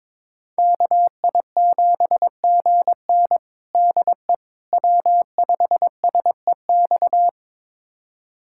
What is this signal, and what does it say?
Morse code, heard as a keyed tone, about 22 wpm, 700 Hz. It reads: KI7GN DE W5SEX
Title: USA Callsign DE Callsign - 5 Characters - 22wpm